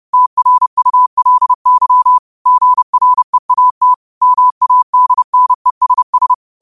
Morse code, heard as a keyed tone, about 30 wpm, 1000 Hz. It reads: TRULY GREAT MADNESS
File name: qotdfiles / 30wpm